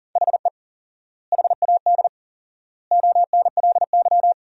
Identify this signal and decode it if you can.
Morse code, heard as a keyed tone, about 40 wpm, 700 Hz. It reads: HE HAD ONLY
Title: 4x Word-Spacing - Sentences from Top 300 Words - 40wpm